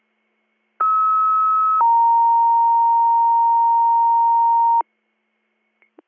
Two-tone page, heard as a one-second tone, then a three-second tone, about 1290 then 930 Hz. It is heard from 0.8 to 4.8 s.